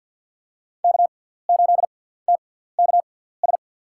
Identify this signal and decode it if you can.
Morse code, heard as a keyed tone, about 46 wpm, spaced as for 17 wpm, 700 Hz. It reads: K8TKS
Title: USA Call Signs - 6x Character Spacing - 50wpm